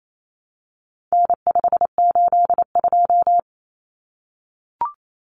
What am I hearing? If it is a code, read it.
Morse code, heard as a keyed tone, about 28 wpm, 700 Hz. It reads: N582
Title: Common Cut Numbers - 4-Digit 28wpm